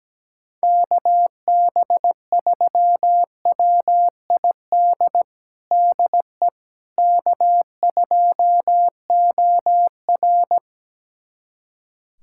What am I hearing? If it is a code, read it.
Morse code, heard as a keyed tone, about 17 wpm, 700 Hz. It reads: KB3WID DE K2OR
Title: USA Callsign DE Callsign 17wpm